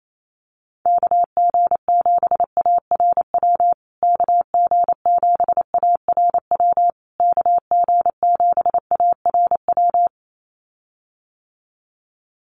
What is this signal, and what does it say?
Morse code, heard as a keyed tone, about 28 wpm, 700 Hz. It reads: KG7ARW KG7ARW KG7ARW
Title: USA Call Signs Repeated 3x 28wpm